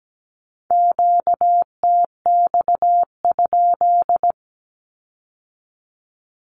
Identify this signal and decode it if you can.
Morse code, heard as a keyed tone, about 17 wpm, 700 Hz. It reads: QTX?
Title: Q-Codes with Spoken Meaning - 17wpm